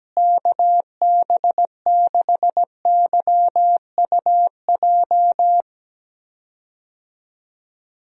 Morse code, heard as a keyed tone, about 17 wpm, 700 Hz. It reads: KB6YUJ
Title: USA Call Signs - 6 Characters - Repeated 2x - 17wpm